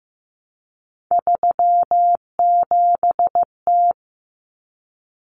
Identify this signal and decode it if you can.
Morse code, heard as a keyed tone, about 15 wpm, 700 Hz. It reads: 37T
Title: Common Cut Numbers - 3-Digit 15wpm